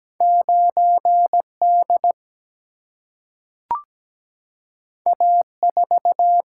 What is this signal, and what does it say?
Morse code, heard as a keyed tone, about 17 wpm, 700 Hz. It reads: 9D A4
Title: Single-Letter Single-Number 17wpm